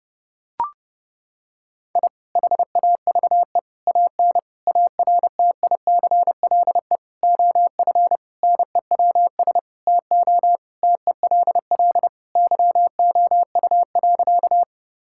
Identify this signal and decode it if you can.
Morse code, heard as a keyed tone, about 30 wpm, 700 Hz. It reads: I HAVE AN ARTICLE OF NEWS TO TELL YOU.
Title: Sentences from Top 1000 Words - Review Part 5 - 30wpm